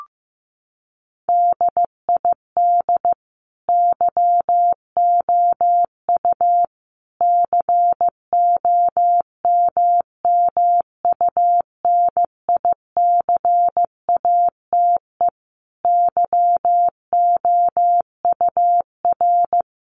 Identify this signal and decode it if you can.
Morse code, heard as a keyed tone, about 15 wpm, 700 Hz. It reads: DID YOU COMMUNICATE YOUR
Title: Sentences from Top 1300 Words 15wpm